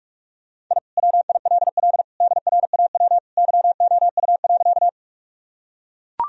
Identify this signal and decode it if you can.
Morse code, heard as a keyed tone, about 41 wpm, 700 Hz. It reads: I WILL DRAW YOU.
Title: Sentences from Top 1000 Words - Review Part 1 - 45wpm